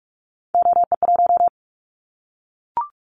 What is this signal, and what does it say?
Morse code, heard as a keyed tone, about 44 wpm, 700 Hz. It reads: OE1
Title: Callsign Prefixes for Top 100 DXCC - 45wpm